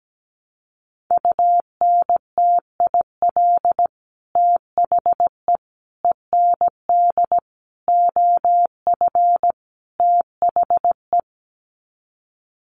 Morse code, heard as a keyed tone, about 17 wpm, 700 Hz. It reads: UNTIL THE END OF THE
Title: Sets of 5 Words Encore 17wpm